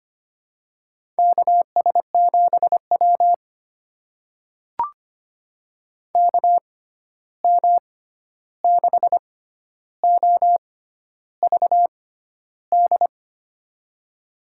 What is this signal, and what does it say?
Morse code, heard as a keyed tone, about 25 wpm, spaced as for 11 wpm, 700 Hz. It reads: KS7W KM6OVD
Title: USA Call Signs - 6x Character Spacing - 25wpm